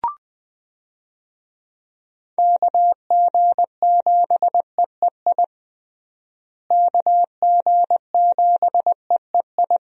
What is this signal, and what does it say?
Morse code, heard as a keyed tone, about 20 wpm, 700 Hz. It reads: KG7EEI KG7EEI
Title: USA Call Signs - 6 Characters - Repeated 2x - 20wpm